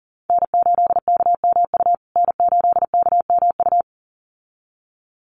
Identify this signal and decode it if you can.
Morse code, heard as a keyed tone, about 40 wpm, 700 Hz. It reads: N8KMU N8KMU
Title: USA Call Signs Repeated 2x 40wpm